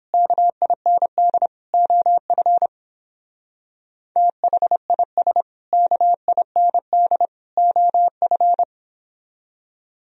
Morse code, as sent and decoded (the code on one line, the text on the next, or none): -.- .. -. -.. / --- ..-. / - .... .. ... / -.- .. -. -.. / --- ..-.
KIND OF THIS KIND OF